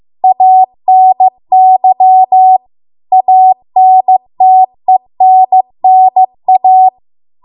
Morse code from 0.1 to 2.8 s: ANY